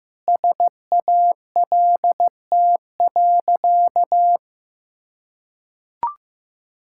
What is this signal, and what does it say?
Morse code, heard as a keyed tone, about 15 wpm, 700 Hz. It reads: SALT.
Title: Sentences from Top 1000 Words - Review Part 5 - 15wpm